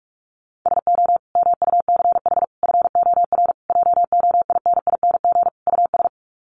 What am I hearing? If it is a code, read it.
Morse code, heard as a keyed tone, about 44 wpm, 700 Hz. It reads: SO MUCH FOR JOINING US
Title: Sets of 5 Words 45wpm